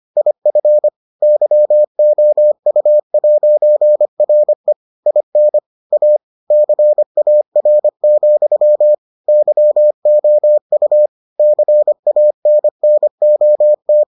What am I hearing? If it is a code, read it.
Morse code, heard as a keyed tone, about 25 wpm, 600 Hz. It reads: IF YOU'RE IN A CAR, YOU CANNOT